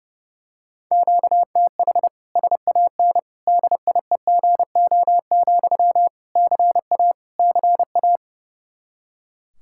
Morse code, heard as a keyed tone, about 30 wpm, 700 Hz. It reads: QTH SAN DIEGO, CA CA
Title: Practice QSO - Exchange 1 - Beginning - 30wpm